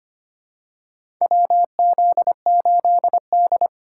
Morse code, heard as a keyed tone, about 25 wpm, 700 Hz. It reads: WZ8D
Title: USA Call Signs - 4x Character Spacing - 25wpm